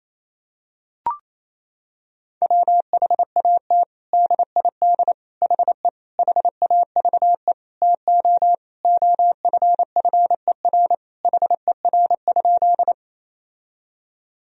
Morse code, heard as a keyed tone, about 28 wpm, 700 Hz. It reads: WHAT DID HE HAVE TO OFFER HER?
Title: Sentences from Top 1000 Words - Review Part 4 - 28wpm